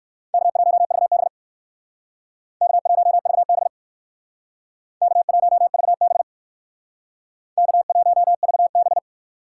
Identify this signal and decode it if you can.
Morse code, heard as a keyed tone, about 63 wpm, 700 Hz. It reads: K1VB K1VB K1VB K1VB